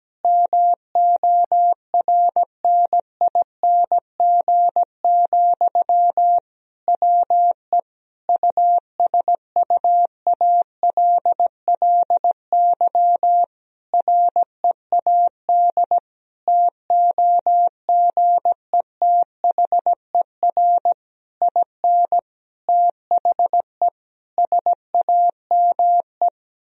Morse code, heard as a keyed tone, about 17 wpm, 700 Hz. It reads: MORNING, WE USUALLY READ TOGETHER IN THE SAME
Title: Sentences from Top 1000 Words - Review Part 3 - 17wpm